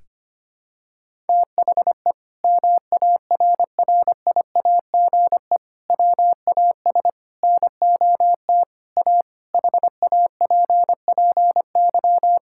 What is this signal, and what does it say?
Morse code, heard as a keyed tone, about 25 wpm, 700 Hz. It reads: THE MARRIAGE WAS NOT A HAPPY
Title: Sentences from Top 1000 Words - Review Part 5 - 25wpm